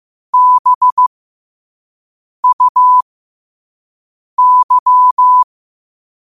Morse code, heard as a keyed tone, about 15 wpm, spaced as for 5 wpm, 1000 Hz. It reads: BUY
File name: qotdfiles / 5wpm